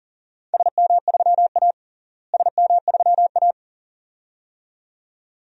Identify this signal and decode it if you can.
Morse code, heard as a keyed tone, about 40 wpm, 700 Hz. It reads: SM3A SM3A
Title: DXCC-100 Call Signs - 4 Characters - Repeated 2x - 40wpm